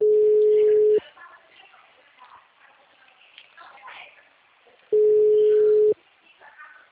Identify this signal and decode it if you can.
Ringback tone (Europe, 425 Hz), with speech.